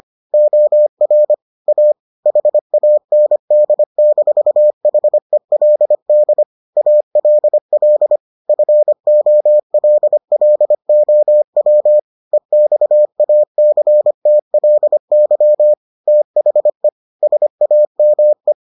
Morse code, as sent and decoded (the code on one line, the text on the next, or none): --- .-. / .- / .... .- -. -.. -....- .... .-.. -.. / .- .-.. .-.. / ..-. --- .-.. .-.. --- .-- / . -..- .- -.-. - .-.. -.-- / - .... / ... .- -- .
OR A HAND-HELD ALL FOLLOW EXACTLY THE SAME